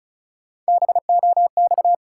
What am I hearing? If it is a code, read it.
Morse code, heard as a keyed tone, about 34 wpm, 700 Hz. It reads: BOX
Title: Sentences from Top 500 Words - New Word Drill - 35wpm